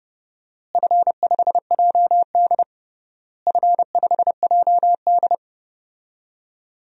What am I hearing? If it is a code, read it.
Morse code, heard as a keyed tone, about 30 wpm, 700 Hz. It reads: F5JD F5JD
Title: DXCC-100 Call Signs - 4 Characters - Repeated 3x - 30wpm